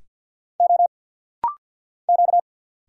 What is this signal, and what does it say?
Morse code, heard as a keyed tone, about 50 wpm, 700 Hz. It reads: O Q
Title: Single Letter-Number Alt - Rapid-Fire 50wpm